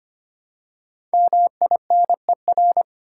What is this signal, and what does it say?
Morse code, heard as a keyed tone, about 25 wpm, 700 Hz. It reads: MINER
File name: 5 Letter Words - First 3 Letters Given - 25wpm